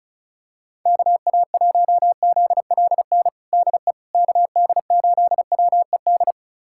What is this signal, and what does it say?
Morse code, heard as a keyed tone, about 36 wpm, 700 Hz. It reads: KA1ZLN DE KD8WED
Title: USA Callsign DE Callsign - 6 Characters - 35wpm